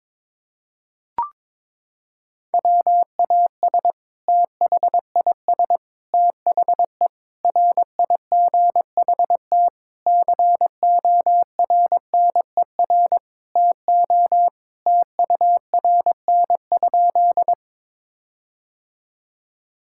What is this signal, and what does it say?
Morse code, heard as a keyed tone, about 22 wpm, 700 Hz. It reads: WAS THIS THE RIGHT CORNER TO TURN?
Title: Sentences from Top 1600 Words 22wpm